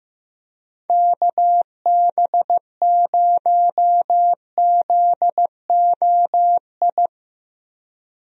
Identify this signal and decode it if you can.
Morse code, heard as a keyed tone, about 15 wpm, 700 Hz. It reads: KB0ZOI
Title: USA Call Signs - 6 Characters - Repeated 2x - 15wpm